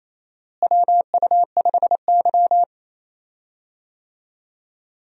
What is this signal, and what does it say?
Morse code, heard as a keyed tone, about 27 wpm, 700 Hz. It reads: WU5Y
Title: USA Call Signs - 4 Characters - Repeated 3x - 28wpm